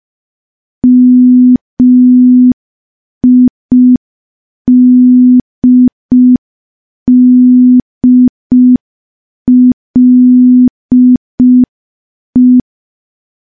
Morse code, heard as a keyed tone, about 5 wpm, 250 Hz. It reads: MIDDLE
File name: morse / morse-training-05WPM